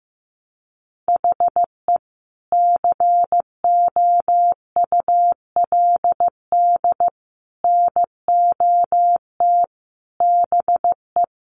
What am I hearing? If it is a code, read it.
Morse code, heard as a keyed tone, about 15 wpm, 700 Hz. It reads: HE COULD NOT BE
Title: Sentences from Top 1400 Words 15wpm